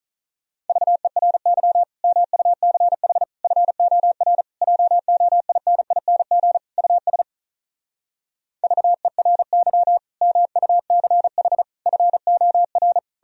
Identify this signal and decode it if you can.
Morse code, heard as a keyed tone, about 46 wpm, 700 Hz. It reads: VERY MUCH FOR JOINING US VERY MUCH FOR